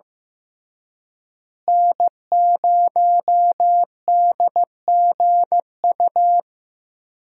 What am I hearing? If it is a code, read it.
Morse code, heard as a keyed tone, about 15 wpm, 700 Hz. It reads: N0DGU
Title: USA Call Signs - 5 Characters - Repeated 3x - 15wpm